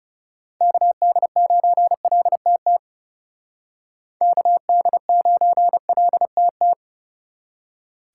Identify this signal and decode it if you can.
Morse code, heard as a keyed tone, about 37 wpm, 700 Hz. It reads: KD9LTT KD9LTT
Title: Speed Racing - USA 2x3 Call Signs 30wpm